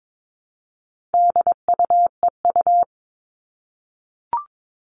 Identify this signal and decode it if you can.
Morse code, heard as a keyed tone, about 22 wpm, 700 Hz. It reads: DUEU
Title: All Cut Numbers - 4-Digit 22wpm